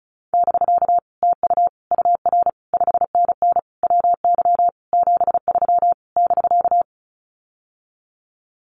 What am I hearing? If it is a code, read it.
Morse code, heard as a keyed tone, about 35 wpm, 700 Hz. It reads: <BK> TU UR 5NN WY 73 <BK>
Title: Common QSO Phrases Encore 35wpm